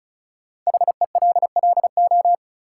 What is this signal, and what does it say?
Morse code, heard as a keyed tone, about 36 wpm, 700 Hz. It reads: HELLO
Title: Sentences from Top 1100 Words - New Word Drill - 35wpm